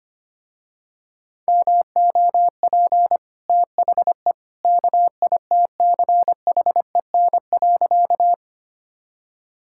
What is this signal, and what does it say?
Morse code, heard as a keyed tone, about 25 wpm, 690 Hz. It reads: MOP THE KITCHEN.
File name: Easy Sentences with Hints - 25wpm